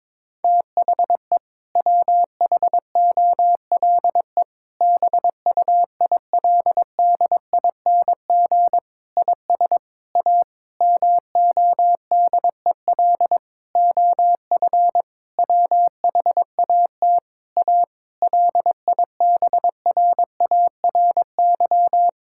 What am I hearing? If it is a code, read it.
Morse code, heard as a keyed tone, about 22 wpm, 700 Hz. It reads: THE WHOLE BUILDING IS A MODEL OF WHAT A LIBRARY
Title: Sentences from Top 1000 Words - Review Part 5 - 22wpm